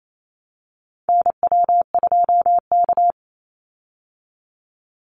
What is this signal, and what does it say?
Morse code, heard as a keyed tone, about 28 wpm, 700 Hz. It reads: NW2K
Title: Call Signs Repeated 2x - 28wpm